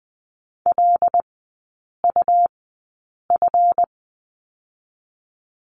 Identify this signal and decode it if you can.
Morse code, heard as a keyed tone, about 20 wpm, spaced as for 8 wpm, 700 Hz. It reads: LUF
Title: 152 Lesson 25e - Letter p - Three Characters 20wpm